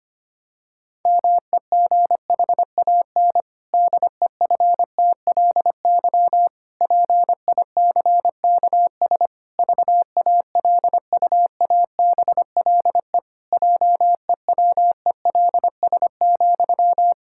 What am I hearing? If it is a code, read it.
Morse code, heard as a keyed tone, about 25 wpm, 700 Hz. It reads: MEGHAN DEFTLY PICKS VALUABLE JEWELS,
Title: Pangram Challenge v2 - 25wpm